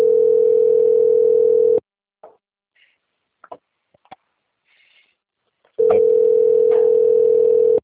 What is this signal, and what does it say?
Ringback tone (North America), 440 + 480 Hz.